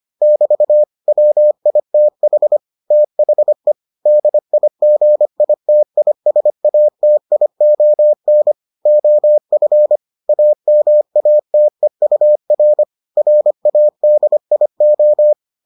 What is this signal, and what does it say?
Morse code, heard as a keyed tone, about 25 wpm, 600 Hz. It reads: <BT> WITH THE DIGITISATION OF AMATEUR RADIO